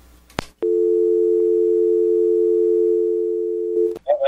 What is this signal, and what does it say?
Dial tone (North America), 350 + 440 Hz.